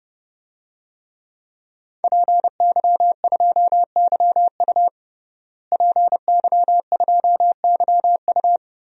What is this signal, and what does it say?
Morse code, heard as a keyed tone, about 30 wpm, 700 Hz. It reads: PY2YU PY2YU
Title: DXCC-100 Call Signs - 5 Characters - Repeated 3x - 30wpm